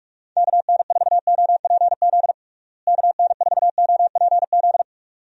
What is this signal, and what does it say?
Morse code, heard as a keyed tone, about 45 wpm, 700 Hz. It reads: KN4OPZ KN4OPZ
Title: USA Call Signs - 6 Characters - Repeated 2x - 45wpm